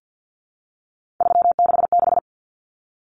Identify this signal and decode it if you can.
Morse code, heard as a keyed tone, about 50 wpm, 700 Hz. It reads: 366
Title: Common Cut Numbers - 3-Digit 50wpm